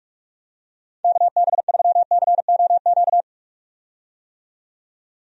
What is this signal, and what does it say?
Morse code, heard as a keyed tone, about 46 wpm, 700 Hz. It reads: KB3COQ
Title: USA Call Signs - 6 Characters - 45wpm